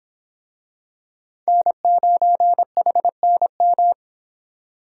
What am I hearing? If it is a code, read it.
Morse code, heard as a keyed tone, about 26 wpm, 700 Hz. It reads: N9HNM